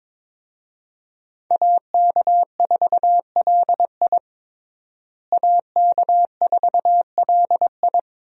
Morse code, heard as a keyed tone, about 22 wpm, 700 Hz. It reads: AK4LI AK4LI
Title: USA Call Signs - 5 Characters - Repeated 3x - 22wpm